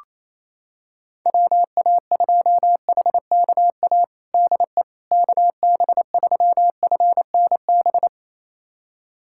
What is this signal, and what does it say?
Morse code, heard as a keyed tone, about 28 wpm, 700 Hz. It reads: WA2HKA DE KB3FNB
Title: USA Callsign DE Callsign - 6 Characters - 28wpm